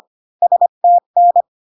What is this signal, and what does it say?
Morse code, heard as a keyed tone, about 25 wpm, 700 Hz. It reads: STN